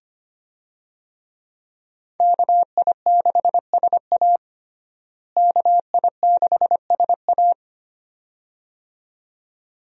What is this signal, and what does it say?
Morse code, heard as a keyed tone, about 25 wpm, 690 Hz. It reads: KI6SA KI6SA
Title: USA Call Signs - 5 Characters - Repeated 2x - 25wpm